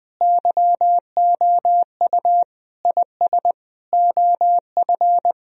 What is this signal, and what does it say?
Morse code, heard as a keyed tone, about 21 wpm, 700 Hz. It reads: YOU IS OF
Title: Sentences from Top 1000 Words - Review Part 3 - 20wpm